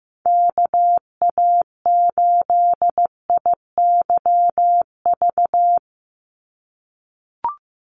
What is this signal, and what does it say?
Morse code, heard as a keyed tone, about 15 wpm, 700 Hz. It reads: KA8IYV
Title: USA Call Signs 15wpm